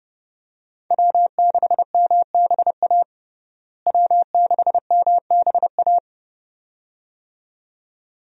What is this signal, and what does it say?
Morse code, heard as a keyed tone, about 32 wpm, 700 Hz. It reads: W6MBA W6MBA
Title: USA Call Signs - 5 Characters - Repeated 2x - 30wpm